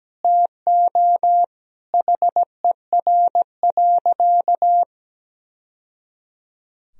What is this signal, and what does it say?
Morse code, heard as a keyed tone, about 17 wpm, 700 Hz. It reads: TO HER.
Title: Sentences from Top 1000 Words - Review Part 2 - 17wpm